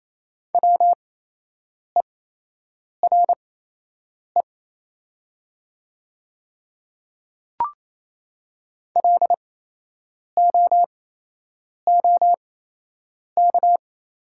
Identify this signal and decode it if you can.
Morse code, heard as a keyed tone, about 28 wpm, spaced as for 8 wpm, 700 Hz. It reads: WERE LOOK
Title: 8x Character Spacing - Top 100 Words - 28wpm